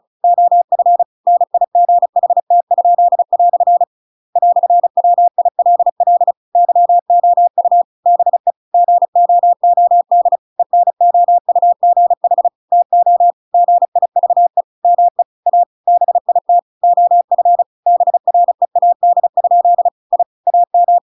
Morse code, heard as a keyed tone, about 34 wpm, 700 Hz. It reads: OF NIGHT?" "WILL YOU BE GOOD ENOUGH TO GIVE ME A BIT OF BREAD? I AM